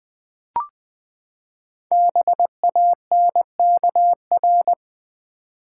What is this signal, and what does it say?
Morse code, heard as a keyed tone, about 20 wpm, 700 Hz. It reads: BANKR
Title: Build A Word - 5 Letters At A Time - 10-Letter Words - 20wpm